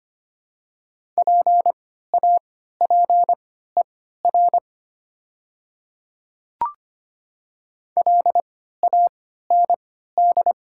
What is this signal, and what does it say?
Morse code, heard as a keyed tone, about 25 wpm, spaced as for 14 wpm, 700 Hz. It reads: PAPER LAND
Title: 3x Character Spacing - Top 300 Words - 25wpm